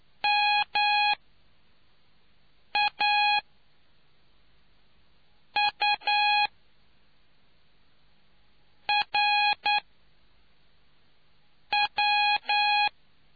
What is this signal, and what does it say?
Morse code, heard as a keyed tone, about 9 wpm, spaced as for 4 wpm, 830 Hz. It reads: MAURW